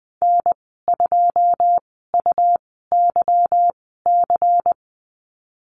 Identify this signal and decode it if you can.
Morse code, heard as a keyed tone, about 20 wpm, spaced as for 14 wpm, 700 Hz. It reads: N2UYC
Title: 193 Lesson 31g - Letter v - Call Signs 20wpm